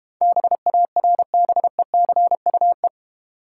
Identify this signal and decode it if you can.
Morse code, heard as a keyed tone, about 32 wpm, 700 Hz. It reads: BARBECUE